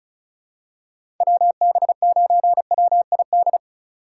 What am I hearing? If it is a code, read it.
Morse code, heard as a keyed tone, about 36 wpm, 700 Hz. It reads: WB9WID
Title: USA Call Signs - 2x Character Spacing - 35wpm